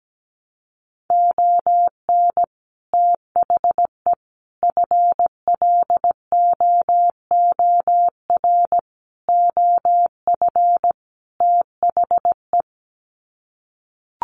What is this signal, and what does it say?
Morse code, heard as a keyed tone, about 17 wpm, 700 Hz. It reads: ON THE FLOOR OF THE
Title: Sets of 5 Words 17wpm